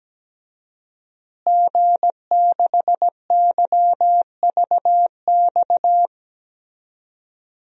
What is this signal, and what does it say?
Morse code, heard as a keyed tone, about 17 wpm, 700 Hz. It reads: G6YVX
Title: DXCC-100 Call Signs - 5 Characters - Repeated 3x - 17wpm